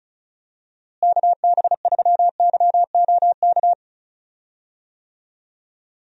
Morse code, heard as a keyed tone, about 36 wpm, 700 Hz. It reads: KB3YOK
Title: USA Call Signs - 6 Characters - Repeated 3x - 35wpm